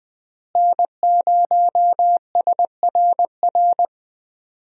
Morse code, heard as a keyed tone, about 20 wpm, 700 Hz. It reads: N0SRR
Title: USA Call Signs - 5 Characters - Repeated 2x - 20wpm